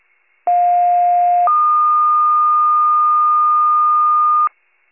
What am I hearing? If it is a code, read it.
Two-tone page, heard as a one-second tone, then a three-second tone, about 710 then 1220 Hz. It is heard from 0.5 to 4.5 s.